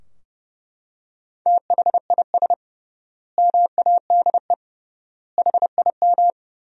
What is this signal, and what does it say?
Morse code, heard as a keyed tone, about 30 wpm, 700 Hz. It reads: THIS MADE HIM
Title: 3x Word-Spacing - Sentences from Top 500 Words - 30wpm